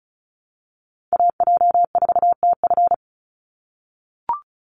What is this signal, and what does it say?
Morse code, heard as a keyed tone, about 34 wpm, 700 Hz. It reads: AJ4TF
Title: USA Call Signs 35wpm